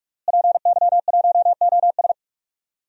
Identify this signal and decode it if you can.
Morse code, heard as a keyed tone, about 45 wpm, 700 Hz. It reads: PY1OS
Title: DXCC-100 Call Signs - 5 Characters - Repeated 3x - 45wpm